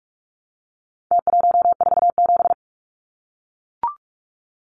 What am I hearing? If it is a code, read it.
Morse code, heard as a keyed tone, about 44 wpm, 700 Hz. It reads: T147
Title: Numbers - Contest Style - 4-Digit 45wpm